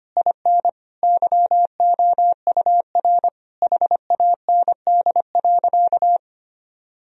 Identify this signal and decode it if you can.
Morse code, heard as a keyed tone, about 25 wpm, 700 Hz. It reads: IN YOUR HAND.
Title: Sentences from Top 1000 Words - Review Part 2 - 25wpm